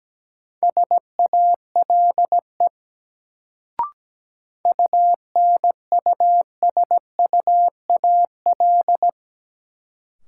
Morse code, heard as a keyed tone, about 17 wpm, 700 Hz. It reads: SALE UNUSUAL
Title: Sentences from Top 1400 Words - New Word Drill - 17wpm